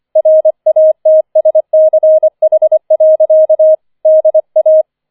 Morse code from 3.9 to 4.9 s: DA